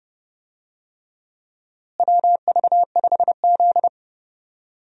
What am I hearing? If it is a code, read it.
Morse code, heard as a keyed tone, about 29 wpm, 700 Hz. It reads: WV5Z
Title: USA Call Signs - 4 Characters - Repeated 3x - 30wpm